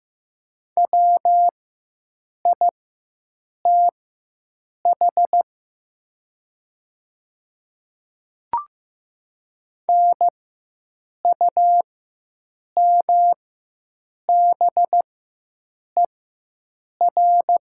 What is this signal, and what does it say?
Morse code, heard as a keyed tone, about 15 wpm, spaced as for 7 wpm, 700 Hz. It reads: WITH NUMBER
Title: 4x Character Spacing - Top 100 Words - 15wpm